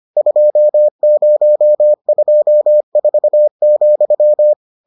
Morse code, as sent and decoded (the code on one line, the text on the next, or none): ..--- ----- ..--- ....- --..--
2024,